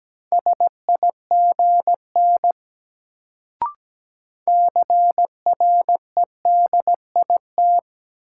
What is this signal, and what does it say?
Morse code, heard as a keyed tone, about 17 wpm, 700 Hz. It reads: SIGN CREDIT
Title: Sentences from Top 1000 Words - New Word Drill - 17wpm